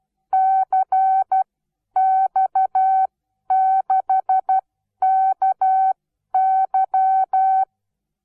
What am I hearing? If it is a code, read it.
Morse code, heard as a keyed tone, about 12 wpm, spaced as for 10 wpm, 760 Hz. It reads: CX6KY
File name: Onlinelehrgang / morsen / DJ4UF Morsen17